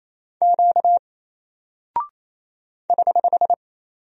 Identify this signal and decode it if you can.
Morse code, heard as a keyed tone, about 27 wpm, 700 Hz. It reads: Q <HH>
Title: Single Character (Letter:Number:Punctuation:Prosign) Alt 28wpm